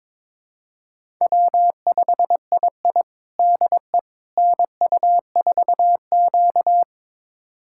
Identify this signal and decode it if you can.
Morse code, heard as a keyed tone, about 22 wpm, 700 Hz. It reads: W5II DE NU4Q
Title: USA Callsign DE Callsign - 4 Characters - 22wpm